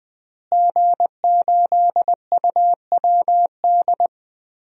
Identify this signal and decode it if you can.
Morse code, heard as a keyed tone, about 20 wpm, 700 Hz. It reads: G8UWD
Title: DXCC-100 Call Signs - 5 Characters - Repeated 2x - 20wpm